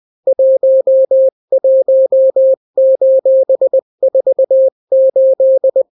Morse code, heard as a keyed tone, about 20 wpm, 530 Hz. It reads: 11:48
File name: daily / 18wpm